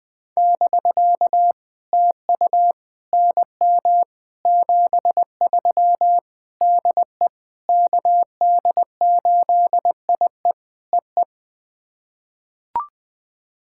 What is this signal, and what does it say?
Morse code, heard as a keyed tone, about 20 wpm, 700 Hz. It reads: <BK> TU NM 73 DE KD8IE EE
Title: POTA QSO Practice - 20wpm